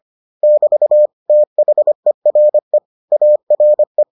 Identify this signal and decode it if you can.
Morse code, heard as a keyed tone, about 25 wpm, 600 Hz. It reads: <BT> THERE ARE